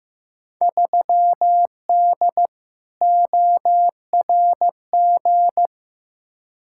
Morse code, heard as a keyed tone, about 15 wpm, 700 Hz. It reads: 3D ORG
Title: Field Day Exchanges Encore - 15wpm